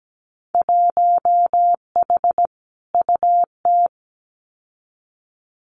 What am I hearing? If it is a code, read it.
Morse code, heard as a keyed tone, about 17 wpm, 700 Hz. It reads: 1H UT
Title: Winter Field Day Exchanges 17wpm